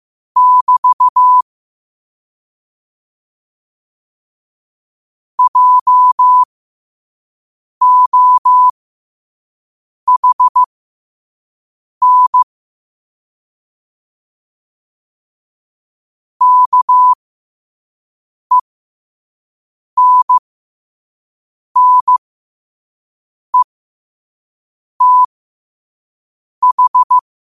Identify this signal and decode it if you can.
Morse code, heard as a keyed tone, about 15 wpm, spaced as for 5 wpm, 1000 Hz. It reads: <BT> JOHN KENNETH
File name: qotdfiles / 5wpm